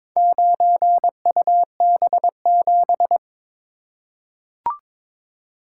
Morse code, heard as a keyed tone, about 22 wpm, 700 Hz. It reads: 9UB7
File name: practice / 22wpm / Four Letter-Number Combinations - 22wpm